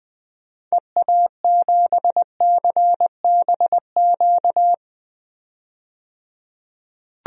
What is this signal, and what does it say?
Morse code, heard as a keyed tone, about 20 wpm, 700 Hz. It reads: EA7CBQ
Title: DXCC-100 Call Signs - 6 Characters - 20wpm